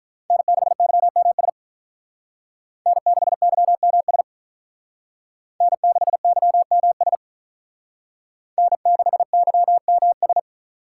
Morse code, heard as a keyed tone, about 45 wpm, 700 Hz. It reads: N6YMS N6YMS N6YMS N6YMS